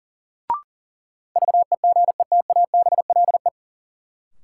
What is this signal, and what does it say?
Morse code, heard as a keyed tone, about 37 wpm, 700 Hz. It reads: VEGETABLE
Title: Sentences from Top 1300 Words - New Word Drill Alt - 40wpm